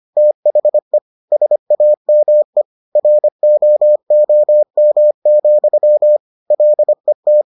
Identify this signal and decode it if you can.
Morse code, heard as a keyed tone, about 25 wpm, 600 Hz. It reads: THE SAME ROOM, LET